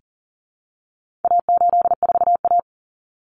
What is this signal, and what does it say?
Morse code, heard as a keyed tone, about 40 wpm, 700 Hz. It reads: A84A
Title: Common Cut Numbers - 4-Digit 40wpm